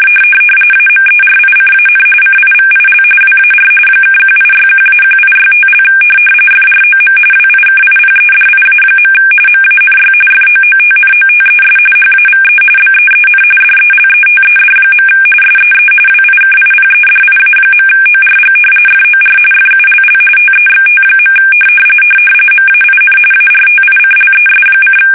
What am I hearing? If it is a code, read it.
RTTY, about 45.45 baud. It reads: AMERICAN CIVIL WAR (1861-1865) FROM WIKIPEDIA. MANY FOOTNOTES HAVE BEEN DELETED. SEE WIKIPEDIA.ORG FOR FULL TEXT. MAIN ARTICLES:
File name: fsk850 / usnavy3